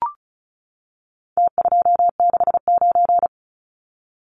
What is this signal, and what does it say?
Morse code, heard as a keyed tone, about 34 wpm, 700 Hz. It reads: T269
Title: Numbers - Contest Style - 4-Digit 35wpm